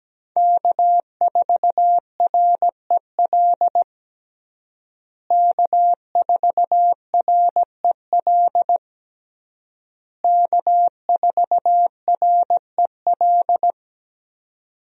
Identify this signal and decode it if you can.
Morse code, heard as a keyed tone, about 17 wpm, 700 Hz. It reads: K4REL K4REL K4REL
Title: USA Call Signs - 5 Characters - Repeated 3x - 17wpm